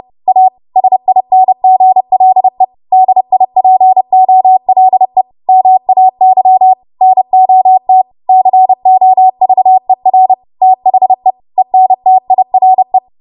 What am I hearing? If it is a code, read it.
Morse code, heard as a keyed tone, about 30 wpm, 750 Hz. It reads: A SINGLE DIPOLE MAY NOT COVER THE ENTIRE
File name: Morse / 30WPM